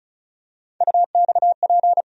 Morse code, heard as a keyed tone, about 34 wpm, 700 Hz. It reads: UXP
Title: USA Call Sign Postfixes - 3 Letters 35wpm